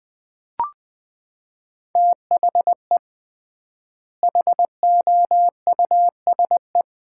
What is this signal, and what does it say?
Morse code, heard as a keyed tone, about 20 wpm, 700 Hz. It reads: THE HOUSE
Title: 3x Word-Spacing - Sets of 2 Words - 20wpm